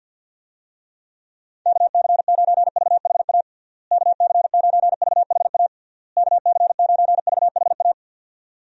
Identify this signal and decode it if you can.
Morse code, heard as a keyed tone, about 52 wpm, 700 Hz. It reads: KC9VHA KC9VHA KC9VHA
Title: USA Call Signs - 6 Characters - Repeated 3x - 50wpm